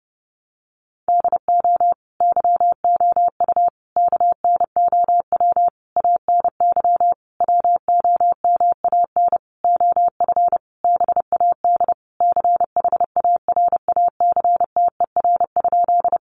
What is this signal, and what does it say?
Morse code, heard as a keyed tone, about 30 wpm, 700 Hz. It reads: DO YOU KNOW ANY WOMAN OF BAD CHARACTER?
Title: Sentences from Top 1200 Words 30wpm